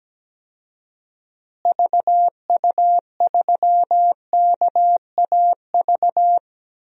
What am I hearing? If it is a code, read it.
Morse code, heard as a keyed tone, about 17 wpm, 700 Hz. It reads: VU3KAV
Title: DXCC-100 Call Signs - 6 Characters - Repeated 3x - 17wpm